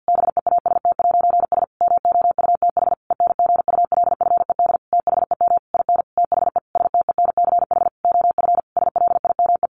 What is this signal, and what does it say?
Morse code, heard as a keyed tone, about 49 wpm, 700 Hz. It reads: BEAST'S MOUTH ENGULFED THEM IN THE STENCH OF SLIME
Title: A Sound Of Thunder 50wpm